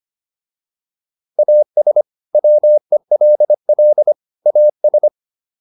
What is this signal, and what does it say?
Morse code, heard as a keyed tone, about 25 wpm, 600 Hz. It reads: AS WELL AS
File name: WD205 25